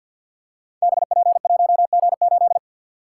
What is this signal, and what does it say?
Morse code, heard as a keyed tone, about 48 wpm, 700 Hz. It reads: BP1G8
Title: Code Groups - 5 Letter:Numbers - 50wpm